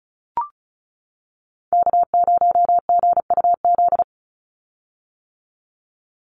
Morse code, heard as a keyed tone, about 34 wpm, 700 Hz. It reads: K0GUZ
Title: Call Signs - 35wpm